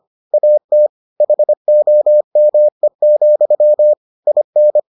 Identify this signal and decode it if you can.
Morse code, heard as a keyed tone, about 25 wpm, 600 Hz. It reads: AT HOME, IN